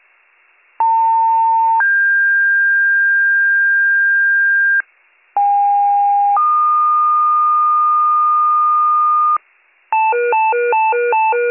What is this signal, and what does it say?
Two-tone page, heard as a one-second tone, then a three-second tone, about 900 then 1600 Hz, about 810 then 1220 Hz, with noise. It is heard from 0.8 to 9.4 s.